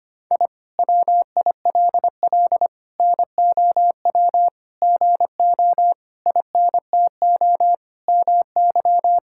Morse code, heard as a keyed tone, about 25 wpm, 700 Hz. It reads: I WILL NOW GO INTO MY
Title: Sentences from Top 1000 Words - Review Part 6 - 25wpm